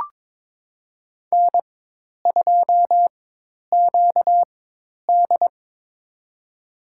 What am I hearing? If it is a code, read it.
Morse code, heard as a keyed tone, about 22 wpm, spaced as for 10 wpm, 700 Hz. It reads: N2QD
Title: USA Call Signs - 4 Characters - 4x Character Spacing - 22wpm